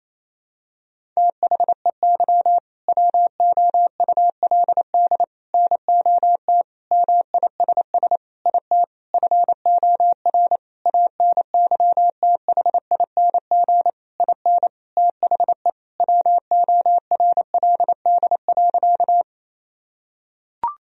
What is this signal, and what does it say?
Morse code, heard as a keyed tone, about 28 wpm, 700 Hz. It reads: THEY WOULD NOT MISS IT FOR ANYTHING IN THE WORLD.
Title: Sentences from Top 1000 Words - Review Part 3 - 28wpm